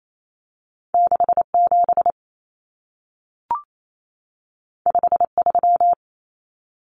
Morse code, heard as a keyed tone, about 28 wpm, 700 Hz. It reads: 67 53
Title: Common Cut Numbers - 2-Digit 28wpm